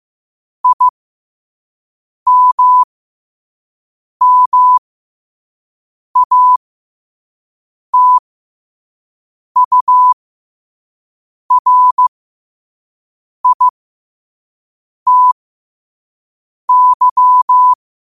Morse code, heard as a keyed tone, about 15 wpm, spaced as for 5 wpm, 1000 Hz. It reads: IMMATURITY
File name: qotdfiles / 5wpm